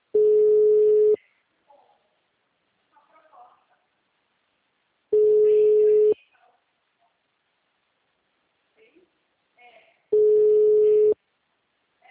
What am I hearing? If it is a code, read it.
Ringback tone (Europe, 425 Hz), with speech.